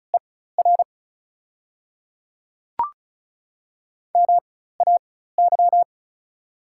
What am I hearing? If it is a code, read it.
Morse code, heard as a keyed tone, about 36 wpm, spaced as for 16 wpm, 690 Hz. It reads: ER MAY
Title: 4x Character Spacing - Top 300 Words - 35wpm